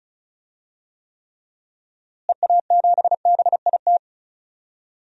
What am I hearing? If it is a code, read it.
Morse code, heard as a keyed tone, about 36 wpm, 700 Hz. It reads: EA7BIT
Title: DXCC-100 Call Signs - 6 Characters - Repeated 2x - 35wpm